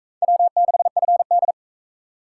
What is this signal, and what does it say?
Morse code, heard as a keyed tone, about 40 wpm, 700 Hz. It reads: W6FD